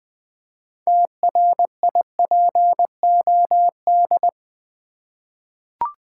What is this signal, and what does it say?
Morse code, heard as a keyed tone, about 20 wpm, 700 Hz. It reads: TRIPOD
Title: Easy Words with Hints - 20wpm